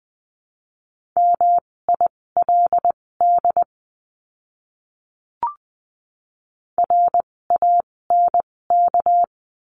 Morse code, heard as a keyed tone, about 20 wpm, spaced as for 16 wpm, 700 Hz. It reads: MILD RANK
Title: 256 Lesson 42a - Words 12-20wpm